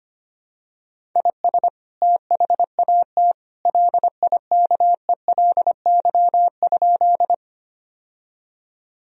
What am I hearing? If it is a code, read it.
Morse code, heard as a keyed tone, about 25 wpm, 700 Hz. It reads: IS THAT LIKELY?
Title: Sentences from Top 1000 Words - Review Part 4 - 25wpm